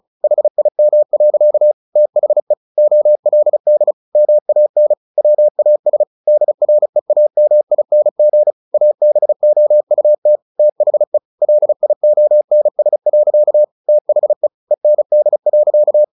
Morse code, heard as a keyed tone, about 34 wpm, 600 Hz. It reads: HIM. THE OLD MAN WAS DREAMING ABOUT THE LIONS. THE END.
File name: the old man and the sea 2 35wpm